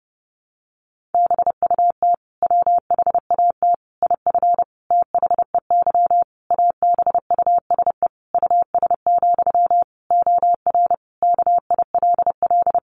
Morse code, heard as a keyed tone, about 30 wpm, 700 Hz. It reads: BUT WHAT IF THEY ABUSE US, OR KILL
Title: Sentences from Top 1400 Words 30wpm